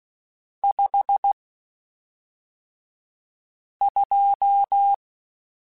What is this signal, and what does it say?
Morse code, heard as a keyed tone, about 16 wpm, spaced as for 3 wpm, 800 Hz. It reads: 52
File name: Spaced Characters / Spaced Characters-mp3s / Weather 4 10 sp @5